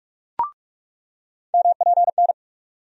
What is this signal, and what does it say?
Morse code, heard as a keyed tone, about 41 wpm, 700 Hz. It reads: MPN
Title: USA Call Sign Postfixes - 3 Letters 45wpm